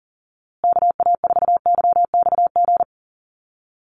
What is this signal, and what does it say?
Morse code, heard as a keyed tone, about 40 wpm, 700 Hz. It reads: KA4YXG
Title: USA Call Signs Repeated 2x 40wpm